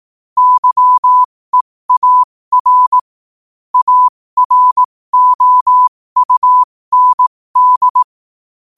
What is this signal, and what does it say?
Morse code, heard as a keyed tone, about 18 wpm, spaced as for 15 wpm, 1000 Hz. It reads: YEAR AROUND
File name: qotdfiles / 15wpm